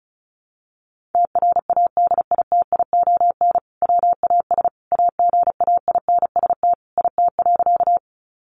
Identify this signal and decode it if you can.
Morse code, heard as a keyed tone, about 34 wpm, 700 Hz. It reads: TRADITION WAS AGAINST IT.
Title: Sentences from Top 1500 Words 35wpm